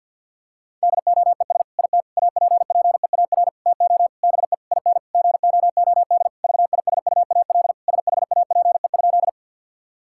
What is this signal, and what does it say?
Morse code, heard as a keyed tone, about 50 wpm, 700 Hz. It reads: DOES IT APPEAR TO BE IN GOOD VISUAL SHAPE?
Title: Sentences from Top 1000 Words - Review Part 6 - 50wpm